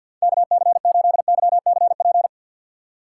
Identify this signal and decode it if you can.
Morse code, heard as a keyed tone, about 49 wpm, 700 Hz. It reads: KC8YCP
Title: USA Call Signs - 6 Characters - 4x Character Spacing - 50wpm